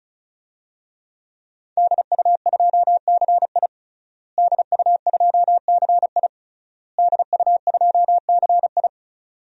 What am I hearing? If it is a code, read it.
Morse code, heard as a keyed tone, about 36 wpm, 700 Hz. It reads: DU2CI DU2CI DU2CI
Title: DXCC-100 Call Signs - 5 Characters - Repeated 3x - 35wpm